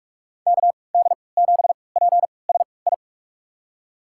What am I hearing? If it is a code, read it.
Morse code, heard as a keyed tone, about 47 wpm, spaced as for 26 wpm, 700 Hz. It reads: KD7PSI
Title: USA Call Signs - 6 Characters - 3x Character Spacing - 45wpm